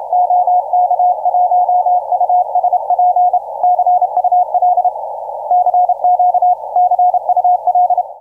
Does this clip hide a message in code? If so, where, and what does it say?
Yes: Morse code from 0.1 to 8.0 s, heard as a keyed tone, about 28 wpm, 720 Hz. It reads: OK1USP CUL CQ CUL